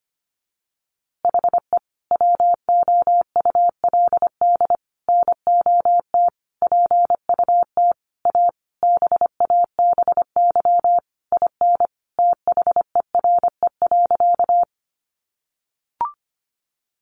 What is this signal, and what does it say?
Morse code, heard as a keyed tone, about 25 wpm, 700 Hz. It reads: HE WOULD NOT PUT A BABY IN THERE.
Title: Easy Phrases 25wpm